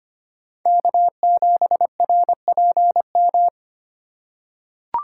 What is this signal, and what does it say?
Morse code, heard as a keyed tone, about 25 wpm, 700 Hz. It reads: K7RPM
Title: USA Call Signs - 5 Characters - 3x Character Spacing - 25wpm